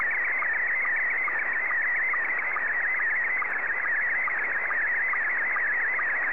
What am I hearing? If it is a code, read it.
SSTV (slow-scan television), Scottie 1.